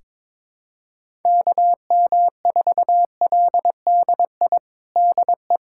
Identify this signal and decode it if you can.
Morse code, heard as a keyed tone, about 22 wpm, 700 Hz. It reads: KM4LDI DE
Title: USA Callsign DE Callsign - 6 Characters - 22wpm